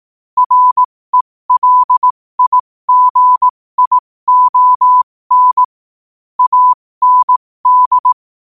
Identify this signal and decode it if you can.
Morse code, heard as a keyed tone, about 18 wpm, spaced as for 15 wpm, 1000 Hz. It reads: RELIGION AND
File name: qotdfiles / 15wpm